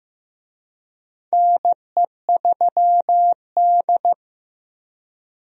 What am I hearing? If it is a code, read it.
Morse code, heard as a keyed tone, about 15 wpm, 700 Hz. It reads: NE3D
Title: USA Call Signs - 4 Characters - Repeated 2x - 15wpm